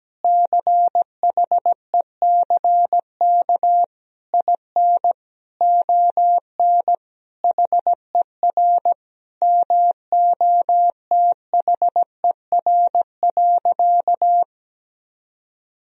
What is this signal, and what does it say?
Morse code, heard as a keyed tone, about 17 wpm, 700 Hz. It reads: CHECK IN ON HER MOTHER.
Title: Sentences from Top 1000 Words - Review Part 3 - 17wpm